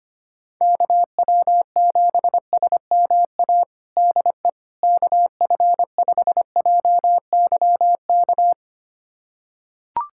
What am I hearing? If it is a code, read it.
Morse code, heard as a keyed tone, about 25 wpm, 700 Hz. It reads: KW7SMA DE KF5JYK
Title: USA Callsign DE Callsign 25wpm